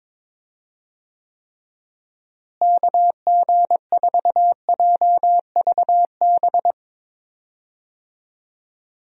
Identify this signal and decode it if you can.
Morse code, heard as a keyed tone, about 22 wpm, 700 Hz. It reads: KG4JVB
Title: USA Call Signs - 6 Characters - Repeated 2x - 22wpm